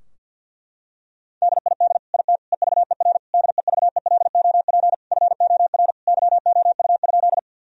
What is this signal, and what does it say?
Morse code, heard as a keyed tone, about 47 wpm, 700 Hz. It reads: DID IT EVER DEVELOP FOR YOU?
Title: Sentences from Top 1000 Words - Review Part 2 - 50wpm